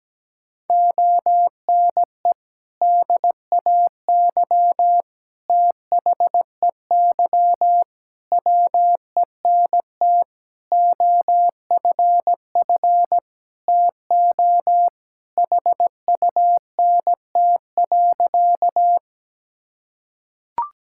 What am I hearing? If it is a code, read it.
Morse code, heard as a keyed tone, about 17 wpm, 700 Hz. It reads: ONE DAY THEY WENT OFF TO HUNT.
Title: Sentences from Top 1600 Words 17wpm